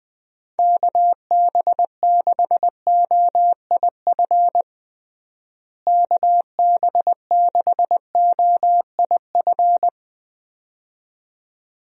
Morse code, heard as a keyed tone, about 20 wpm, 700 Hz. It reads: KB6OIF KB6OIF
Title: USA Call Signs - 6 Characters - Repeated 2x - 20wpm